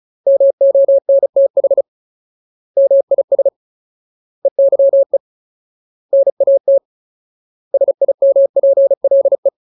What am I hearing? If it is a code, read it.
Morse code, heard as a keyed tone, about 34 wpm, 550 Hz. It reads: MONTH MIS EYE NAT SIMPLE
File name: Ultimate Word List / 35WPM Ultimate